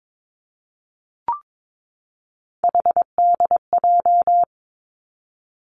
Morse code, heard as a keyed tone, about 22 wpm, 700 Hz. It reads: HDJ
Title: Three Challenging Letter-Number Combinations 22wpm